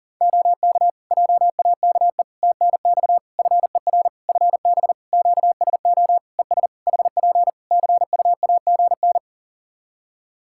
Morse code, heard as a keyed tone, about 40 wpm, 700 Hz. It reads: OK JAKE TNX FER FB QSO ES HP CUAGN
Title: Practice QSO - Exchange 4 - Ending - 40wpm